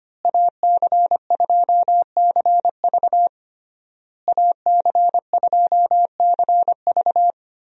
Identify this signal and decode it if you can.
Morse code, heard as a keyed tone, about 25 wpm, 700 Hz. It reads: AC2CV AC2CV
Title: USA Call Signs - 5 Characters - Repeated 3x - 25wpm